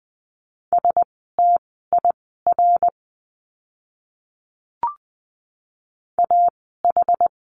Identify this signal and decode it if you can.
Morse code, heard as a keyed tone, about 20 wpm, spaced as for 14 wpm, 700 Hz. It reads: STIR AH
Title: 061 Lesson 11f - Letter h - Words 20wpm